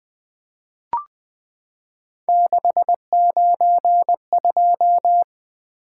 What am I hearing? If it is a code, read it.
Morse code, heard as a keyed tone, about 20 wpm, 700 Hz. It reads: 692
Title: Three Letter-Number Combinations - 20wpm